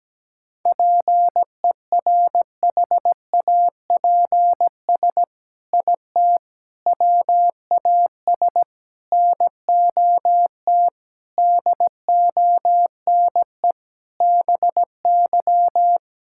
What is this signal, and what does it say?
Morse code, heard as a keyed tone, about 17 wpm, 700 Hz. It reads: PERHAPS IT WAS NOT DONE BY
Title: Sentences from Top 1000 Words - Review Part 5 - 17wpm